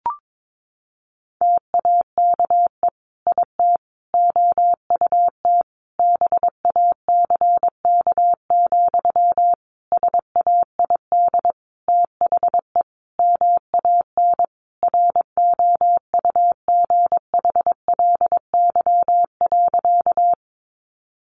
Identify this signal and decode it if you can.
Morse code, heard as a keyed tone, about 22 wpm, 700 Hz. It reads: TAKE IT OUT BACK, SAID THE MAN ROUGHLY.
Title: Sentences from Top 1500 Words 22wpm